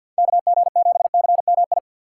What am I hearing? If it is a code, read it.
Morse code, heard as a keyed tone, about 53 wpm, 700 Hz. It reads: KG7CMI
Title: USA Call Signs - 6 Characters - 4x Character Spacing - 50wpm